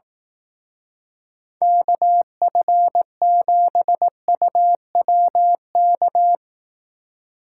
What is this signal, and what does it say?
Morse code, heard as a keyed tone, about 18 wpm, 700 Hz. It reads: KF7UWK
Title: Speed Racing - USA 2x3 Call Signs 15wpm